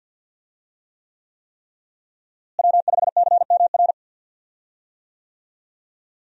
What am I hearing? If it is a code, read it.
Morse code, heard as a keyed tone, about 48 wpm, 700 Hz. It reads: W5CMR
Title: USA Call Signs - 5 Characters - Repeated 3x - 50wpm